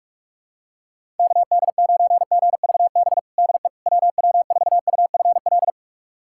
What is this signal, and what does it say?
Morse code, heard as a keyed tone, about 41 wpm, 700 Hz. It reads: KD9GVB DE WW4UFL
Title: USA Callsign DE Callsign - 6 Characters - 45wpm